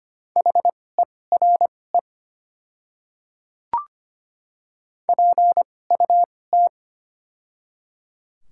Morse code, heard as a keyed tone, about 25 wpm, spaced as for 18 wpm, 700 Hz. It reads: HERE PUT
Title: 2x Character Spacing - Top 200 Words - 25wpm